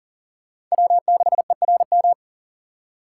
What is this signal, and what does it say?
Morse code, heard as a keyed tone, about 40 wpm, 700 Hz. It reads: W6ERM
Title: USA Call Signs - 5 Characters - 2x Character Spacing - 40wpm